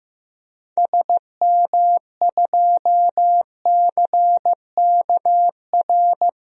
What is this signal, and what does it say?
Morse code, heard as a keyed tone, about 15 wpm, 700 Hz. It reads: SM2CKR
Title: DXCC-100 Call Signs - 6 Characters - 15wpm